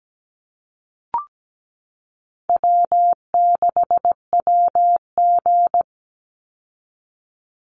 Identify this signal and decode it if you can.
Morse code, heard as a keyed tone, about 17 wpm, 700 Hz. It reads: W6WG
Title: Call Signs Repeated 2x - 17wpm